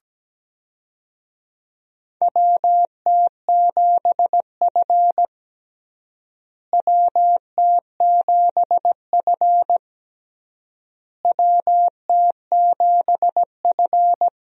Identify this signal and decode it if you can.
Morse code, heard as a keyed tone, about 17 wpm, 700 Hz. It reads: WT7F WT7F WT7F
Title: USA Call Signs - 4 Characters - Repeated 3x - 17wpm